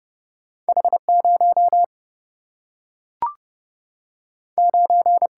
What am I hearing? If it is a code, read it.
Morse code, heard as a keyed tone, about 31 wpm, 700 Hz. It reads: H 0 9
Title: Single-Letter Single-Number 30wpm